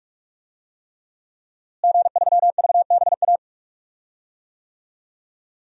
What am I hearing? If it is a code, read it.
Morse code, heard as a keyed tone, about 42 wpm, 700 Hz. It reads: G3VBA
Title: DXCC-100 Call Signs - 5 Characters - Repeated 3x - 45wpm